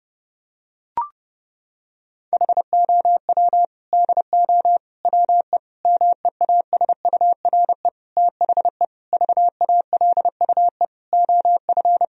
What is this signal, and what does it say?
Morse code, heard as a keyed tone, about 32 wpm, 700 Hz. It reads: HOW DO WE MEASURE THE VALUE OF
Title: Sentences from Top 1000 Words - Review Part 3 - 30wpm